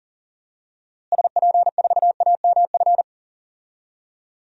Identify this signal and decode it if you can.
Morse code, heard as a keyed tone, about 40 wpm, 700 Hz. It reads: SP4AMF
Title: DXCC-100 Call Signs - 6 Characters - Repeated 2x - 40wpm